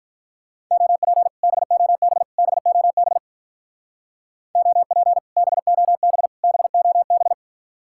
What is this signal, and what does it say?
Morse code, heard as a keyed tone, about 46 wpm, 700 Hz. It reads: OP BOB BOB OP BOB BOB